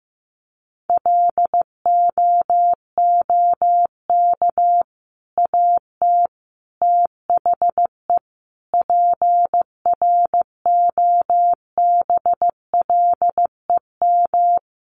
Morse code, heard as a keyed tone, about 14 wpm, 700 Hz. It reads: LOOK AT THE PROBLEM
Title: Sentences from Top 1500 Words 15wpm